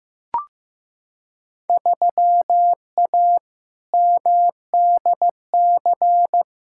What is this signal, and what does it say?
Morse code, heard as a keyed tone, about 15 wpm, 700 Hz. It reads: 3A MDC
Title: Field Day Exchanges - 15wpm